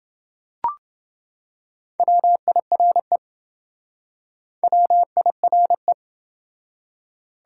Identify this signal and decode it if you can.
Morse code, heard as a keyed tone, about 27 wpm, 700 Hz. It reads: WIRE WIRE